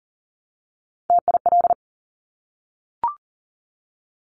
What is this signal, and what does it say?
Morse code, heard as a keyed tone, about 38 wpm, 700 Hz. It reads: TIL
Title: Top Three Letter Combinations in Words 40wpm